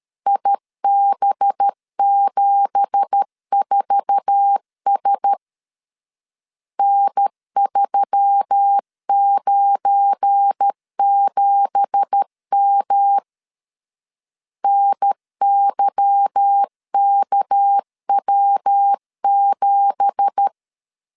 Morse code, heard as a keyed tone, about 13 wpm, 800 Hz. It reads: IB74S N397M NYKW7
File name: LEZIONI / 9 2867 12wpm 2322